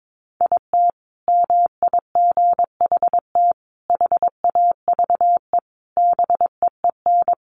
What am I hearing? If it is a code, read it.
Morse code, heard as a keyed tone, about 22 wpm, 700 Hz. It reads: IT MIGHT HAVE BEEN
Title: Sentences from Top 1200 Words 22wpm